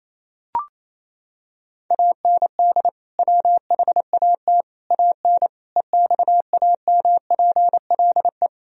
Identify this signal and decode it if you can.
Morse code, heard as a keyed tone, about 28 wpm, 700 Hz. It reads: AND WHAT AN EXAMPLE
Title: Sentences from Top 1000 Words - Review Part 1 - 28wpm